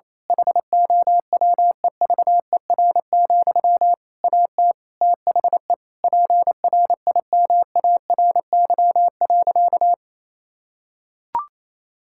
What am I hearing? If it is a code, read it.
Morse code, heard as a keyed tone, about 28 wpm, 700 Hz. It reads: HOWEVER, AT THE PRIMARY.
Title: Sentences from Top 1000 Words - Review Part 5 - 28wpm